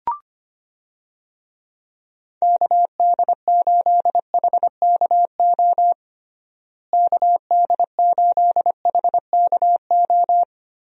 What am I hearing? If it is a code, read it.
Morse code, heard as a keyed tone, about 25 wpm, 700 Hz. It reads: KD8HKO KD8HKO
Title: USA Call Signs - 6 Characters - Repeated 3x - 25wpm